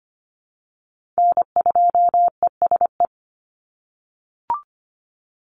Morse code, heard as a keyed tone, about 25 wpm, 700 Hz. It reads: N2ESE
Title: NA SOTA Chaser Callsigns 25wpm